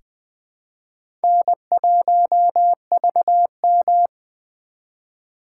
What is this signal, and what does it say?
Morse code, heard as a keyed tone, about 20 wpm, 700 Hz. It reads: N1VM
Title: USA Call Signs - 3x Character Spacing - 20wpm